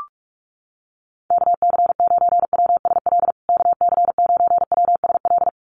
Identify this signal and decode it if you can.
Morse code, heard as a keyed tone, about 44 wpm, 700 Hz. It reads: KC9WSL KC9WSL
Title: USA Call Signs Repeated 2x 45wpm